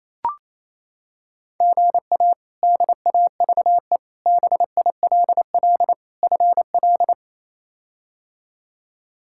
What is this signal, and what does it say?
Morse code, heard as a keyed tone, about 28 wpm, 700 Hz. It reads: GA DAVE BILL FL
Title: K1USN SST QSO Practice - 28wpm